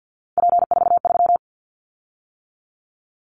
Morse code, heard as a keyed tone, about 50 wpm, 700 Hz. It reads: P43
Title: Callsign Prefixes for Top 100 DXCC - 50wpm